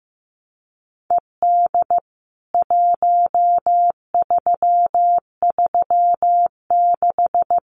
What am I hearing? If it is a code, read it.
Morse code, heard as a keyed tone, about 15 wpm, 700 Hz. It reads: ED 1336
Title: CWT Exchange 15wpm